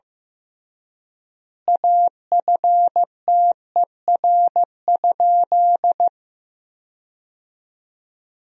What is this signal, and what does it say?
Morse code, heard as a keyed tone, about 15 wpm, 700 Hz. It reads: AFTER?
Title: 3x Word-Spacing - Sentences from Top 200 Words - 15wpm